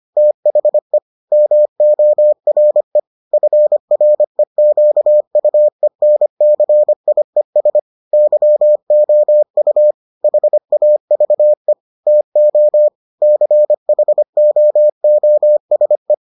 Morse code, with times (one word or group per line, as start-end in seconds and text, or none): 0.2-1.0 THE
1.3-3.0 MORE
3.3-7.8 FREQUENCIES
8.1-9.9 YOU
10.2-11.7 HAVE
12.1-12.9 TO
13.2-16.1 CHOOSE